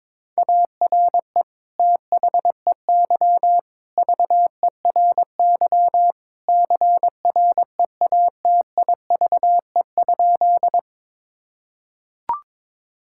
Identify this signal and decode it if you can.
Morse code, heard as a keyed tone, about 22 wpm, 700 Hz. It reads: ARE THEY VERY CREATIVE?
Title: Sentences from Top 1000 Words - Review Part 5 - 22wpm